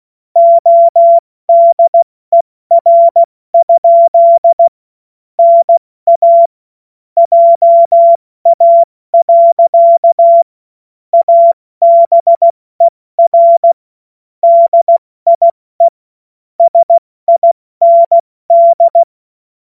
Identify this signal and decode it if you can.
Morse code, heard as a keyed tone, about 16 wpm, 680 Hz. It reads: ODER? NA JA. ABER DIE SIND
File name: Morsekurs / Witze / Witz 18. Januar 2022, 70 F80